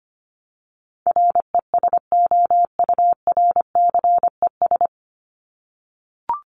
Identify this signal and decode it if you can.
Morse code, heard as a keyed tone, about 25 wpm, 700 Hz. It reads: RESOURCES
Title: ES:S Suffix - Top 100 Words with Base Words 25wpm